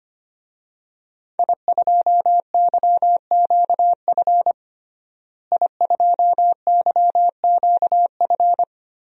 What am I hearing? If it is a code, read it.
Morse code, heard as a keyed tone, about 25 wpm, 700 Hz. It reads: I2YQF I2YQF
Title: DXCC-100 Call Signs - 5 Characters - Repeated 3x - 25wpm